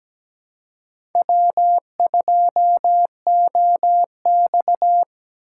Morse code, heard as a keyed tone, about 17 wpm, 700 Hz. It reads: W2OX
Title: USA Call Signs - 4 Characters - 2x Character Spacing - 17wpm